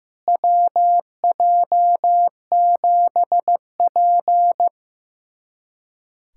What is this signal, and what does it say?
Morse code, heard as a keyed tone, about 15 wpm, 700 Hz. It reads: WJ7P
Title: USA Callsign DE Callsign - 4 Characters - 15wpm